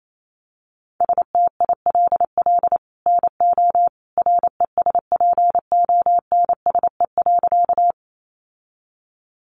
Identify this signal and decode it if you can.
Morse code, heard as a keyed tone, about 28 wpm, 700 Hz. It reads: STILL NO RESPONSE.
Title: Sentences from Top 1400 Words 28wpm